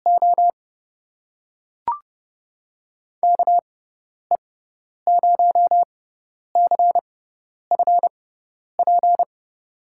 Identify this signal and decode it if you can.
Morse code, heard as a keyed tone, about 30 wpm, spaced as for 10 wpm, 700 Hz. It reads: O KE0CFP
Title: USA Call Signs - 6 Characters - 6x Character Spacing - 30wpm